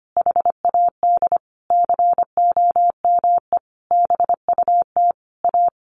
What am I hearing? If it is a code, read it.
Morse code, heard as a keyed tone, about 25 wpm, 700 Hz. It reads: HAD COME BUT A
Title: Sentences from Top 1400 Words 25wpm